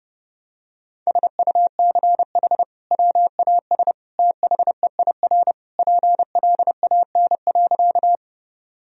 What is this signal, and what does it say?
Morse code, heard as a keyed tone, about 32 wpm, 700 Hz. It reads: SUCH WAS THEIR PLAN.
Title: Sentences from Top 1000 Words - Review Part 2 - 30wpm